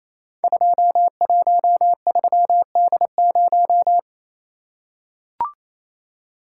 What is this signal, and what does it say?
Morse code, heard as a keyed tone, about 28 wpm, 700 Hz. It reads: 213D0
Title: Code Groups - 5 Letter:Numbers - 28wpm